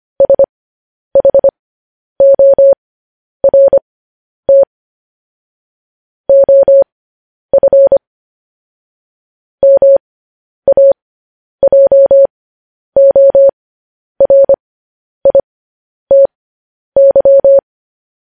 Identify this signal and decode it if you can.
Morse code, heard as a keyed tone, about 25 wpm, spaced as for 10 wpm, 550 Hz. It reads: SHORT OF MAJORITY
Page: news-headlines-in-morse-code-at-10-wpm